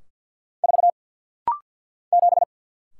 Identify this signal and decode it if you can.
Morse code, heard as a keyed tone, about 50 wpm, 690 Hz. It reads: 4 7
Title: Single Letter-Number Alt - Rapid-Fire 50wpm